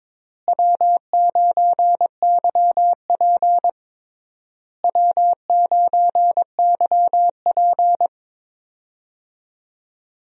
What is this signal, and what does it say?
Morse code, heard as a keyed tone, about 21 wpm, 700 Hz. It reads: W9YP W9YP
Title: USA Call Signs - 4 Characters - Repeated 3x - 22wpm